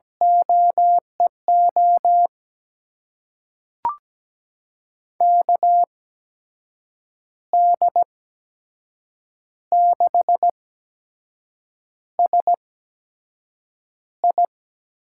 Morse code, heard as a keyed tone, about 17 wpm, spaced as for 6 wpm, 700 Hz. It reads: OEO KD6SI
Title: USA Call Signs - 6 Characters - 8x Character Spacing - 17wpm